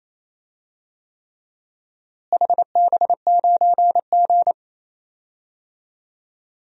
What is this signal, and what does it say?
Morse code, heard as a keyed tone, about 28 wpm, 700 Hz. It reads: HB9G
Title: DXCC-100 Call Signs - 4 Characters - Repeated 3x - 28wpm